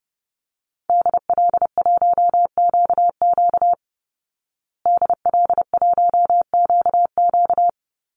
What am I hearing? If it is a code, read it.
Morse code, heard as a keyed tone, about 30 wpm, 700 Hz. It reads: DL1QQ DL1QQ
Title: Call Signs Repeated 3x - 30wpm